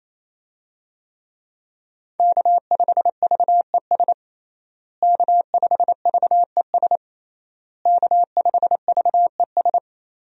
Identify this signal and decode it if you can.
Morse code, heard as a keyed tone, about 27 wpm, 700 Hz. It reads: K5VES K5VES K5VES
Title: USA Call Signs - 5 Characters - Repeated 3x - 28wpm